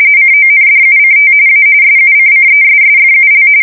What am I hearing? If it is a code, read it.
RTTY, about 45.45 baud. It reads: CONDUCT A SUCCESSFUL
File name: fsk170 / battleshipnj